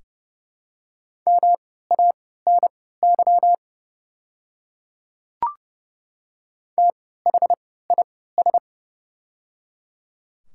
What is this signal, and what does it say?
Morse code, heard as a keyed tone, about 31 wpm, spaced as for 17 wpm, 700 Hz. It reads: MANY THIS
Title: 3x Character Spacing - Top 100 Words - 30wpm